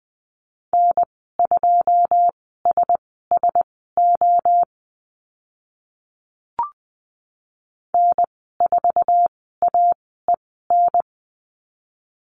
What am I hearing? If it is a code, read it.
Morse code, heard as a keyed tone, about 20 wpm, spaced as for 14 wpm, 700 Hz. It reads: N2SSO N4AEN
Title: 082 Lesson 14g - Number 2 - Call Signs 20wpm